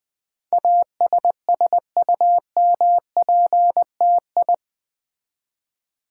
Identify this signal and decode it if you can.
Morse code, heard as a keyed tone, about 20 wpm, 700 Hz. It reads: ASSUMPTI
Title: Build A Word - 4 Letters At A Time - 10-Letter Words - 20wpm